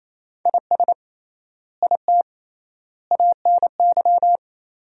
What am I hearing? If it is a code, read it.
Morse code, heard as a keyed tone, about 28 wpm, 700 Hz. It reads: IS IT ANY
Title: 3x Word-Spacing - Sentences from Top 200 Words - 28wpm